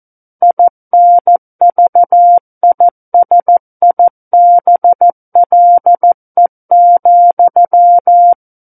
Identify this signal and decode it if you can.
Morse code, heard as a keyed tone, about 14 wpm, 700 Hz. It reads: INVISIBLE,
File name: Book One / 05-Chapter 5 - 14 WPM